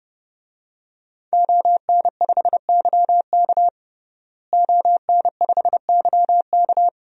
Morse code, heard as a keyed tone, about 30 wpm, 700 Hz. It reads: ON5YK ON5YK
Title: DXCC-100 Call Signs - 5 Characters - Repeated 2x - 30wpm